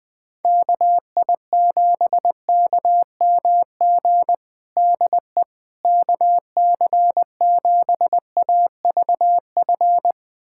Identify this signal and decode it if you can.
Morse code, heard as a keyed tone, about 20 wpm, 700 Hz. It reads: KI7KMG DE KC7AVF
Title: USA Callsign DE Callsign - 6 Characters - 20wpm